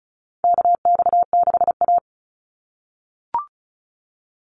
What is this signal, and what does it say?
Morse code, heard as a keyed tone, about 34 wpm, 700 Hz. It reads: KX6A
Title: NA SOTA Chaser Callsigns 35wpm